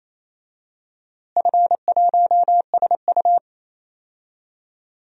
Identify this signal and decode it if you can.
Morse code, heard as a keyed tone, about 28 wpm, 700 Hz. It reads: F1SU
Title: DXCC-100 Call Signs - 4 Characters - Repeated 2x - 28wpm